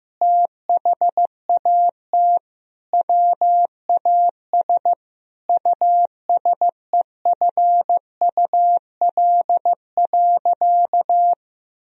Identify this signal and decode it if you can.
Morse code, heard as a keyed tone, about 15 wpm, 700 Hz. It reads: THAT WAS USEFUL.
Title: Sentences from Top 1000 Words - Review Part 4 - 15wpm